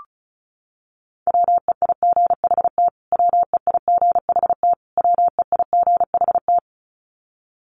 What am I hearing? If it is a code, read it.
Morse code, heard as a keyed tone, about 34 wpm, 700 Hz. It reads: WEIGHT WEIGHT WEIGHT
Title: Top 500 Words Repeated 35wpm